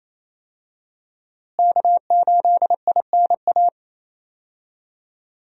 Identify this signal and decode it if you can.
Morse code, heard as a keyed tone, about 28 wpm, 700 Hz. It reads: K8INA
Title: USA Call Signs - 5 Characters - Repeated 2x - 28wpm